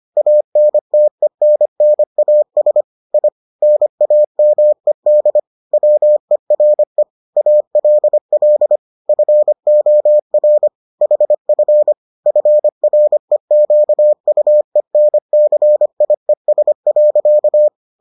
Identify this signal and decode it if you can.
Morse code, heard as a keyed tone, about 25 wpm, 600 Hz. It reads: ANTENNAS I NAMED WERE ALL FOR HF FREQUENCIES.